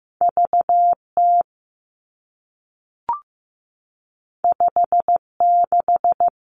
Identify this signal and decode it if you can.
Morse code, heard as a keyed tone, about 15 wpm, 700 Hz. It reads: V T 5 6
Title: Two Challenging Letter-Number Combinations 15wpm